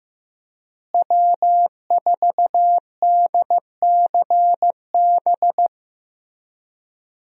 Morse code, heard as a keyed tone, about 15 wpm, 700 Hz. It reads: W4DCB
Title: USA Call Signs - 5 Characters - Repeated 3x - 15wpm